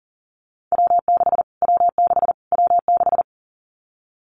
Morse code, heard as a keyed tone, about 40 wpm, 700 Hz. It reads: W6 W6 W6
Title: Callsign Prefixes for Top 25 DXCC - Repeated 3x - 40wpm